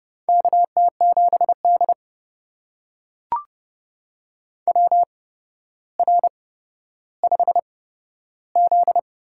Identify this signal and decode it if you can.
Morse code, heard as a keyed tone, about 30 wpm, spaced as for 13 wpm, 700 Hz. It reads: KT7D WR5Z
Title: USA Call Signs - 4 Characters - 8x Character Spacing - 30wpm